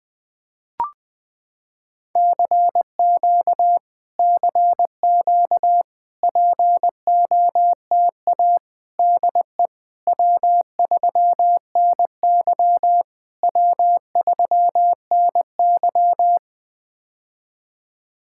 Morse code, heard as a keyed tone, about 20 wpm, 700 Hz. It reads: CQ CQ POTA DE W3NY W3NY
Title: POTA QSO Practice - 20wpm